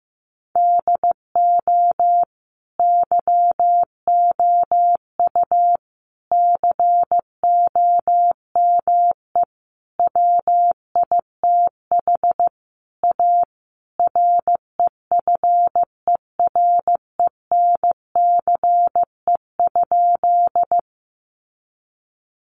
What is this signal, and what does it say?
Morse code, heard as a keyed tone, about 15 wpm, 700 Hz. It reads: DO YOU COME WITH A REFERENCE?
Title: Sentences from Top 1200 Words 15wpm